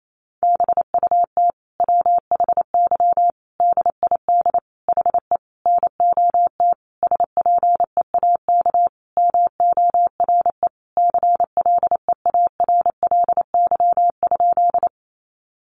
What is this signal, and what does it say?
Morse code, heard as a keyed tone, about 28 wpm, 700 Hz. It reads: BUT WHY DID HE NOT SPEAK MORE CLEARLY?
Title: Sentences from Top 1300 Words 28wpm